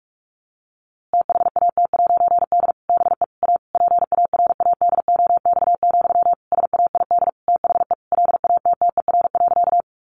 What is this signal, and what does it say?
Morse code, heard as a keyed tone, about 44 wpm, 700 Hz. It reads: THAT'D BE A PARADOX, SAID THE LATTER.
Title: A Sound Of Thunder 45wpm